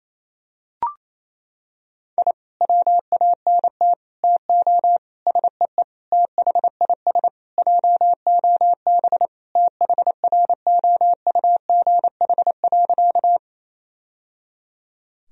Morse code, heard as a keyed tone, about 27 wpm, 700 Hz. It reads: I WANT TO SEE THIS JOB THROUGH.
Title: Sentences from Top 1000 Words - Review Part 3 - 28wpm